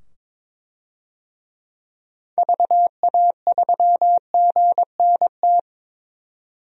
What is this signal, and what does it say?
Morse code, heard as a keyed tone, about 22 wpm, 700 Hz. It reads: VA3GNT
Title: DXCC-100 Call Signs - 6 Characters - 22wpm